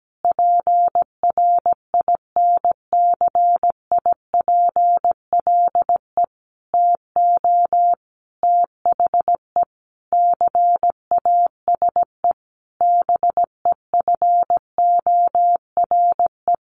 Morse code, heard as a keyed tone, about 17 wpm, 700 Hz. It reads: PRINCIPLE TO THE CASE BEFORE
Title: Sentences from Top 1400 Words 17wpm